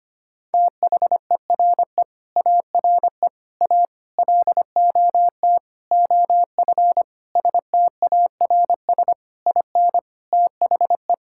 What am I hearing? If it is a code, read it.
Morse code, heard as a keyed tone, about 25 wpm, 700 Hz. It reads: THERE ARE A LOT OF STARS IN THE
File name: practice / 25wpm / Sentences with 4-Word Phrases - 25wpm